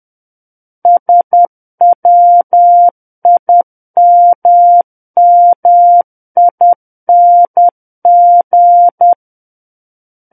Morse code, heard as a keyed tone, about 10 wpm, 700 Hz. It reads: SWIMMING
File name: Book One / 01-Chapter 1 - 10 WPM